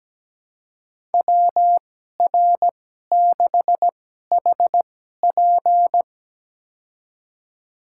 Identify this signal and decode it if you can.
Morse code, heard as a keyed tone, about 17 wpm, spaced as for 12 wpm, 700 Hz. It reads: WR6HP
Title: USA Call Signs - 5 Characters - 2x Character Spacing - 17wpm